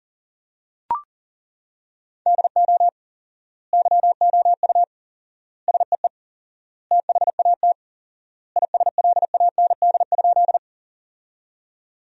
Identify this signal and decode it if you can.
Morse code, heard as a keyed tone, about 37 wpm, 700 Hz. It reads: DO YOU SEE THAT ISLAND?
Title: 4x Word-Spacing - Sentences from Top 500 Words - 40wpm